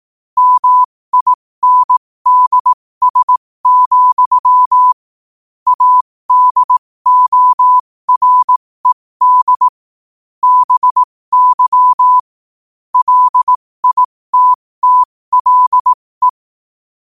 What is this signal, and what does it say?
Morse code, heard as a keyed tone, about 18 wpm, spaced as for 15 wpm, 1000 Hz. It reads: MINDS, ADORED BY LITTLE
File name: qotdfiles / 15wpm